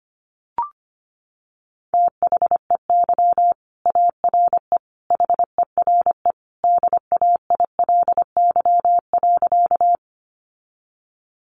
Morse code, heard as a keyed tone, about 25 wpm, 700 Hz. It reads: THEY ARE HERE DAILY.
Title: Sentences from Top 1100 Words 25wpm